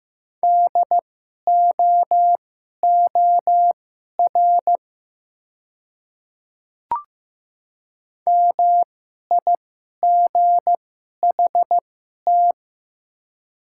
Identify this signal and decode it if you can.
Morse code, heard as a keyed tone, about 15 wpm, spaced as for 11 wpm, 700 Hz. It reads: DOOR MIGHT
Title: 2x Character Spacing - Top 300 Words - 15wpm